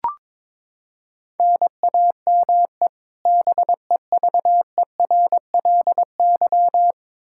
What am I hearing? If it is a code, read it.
Morse code, heard as a keyed tone, about 21 wpm, 700 Hz. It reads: NAME BEVERLY
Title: Practice QSO - Exchange 1 - Beginning - 22wpm